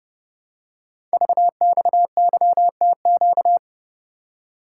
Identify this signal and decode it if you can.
Morse code, heard as a keyed tone, about 29 wpm, 700 Hz. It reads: VXYTQ
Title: Code Groups - 5 Letter:Numbers - 30wpm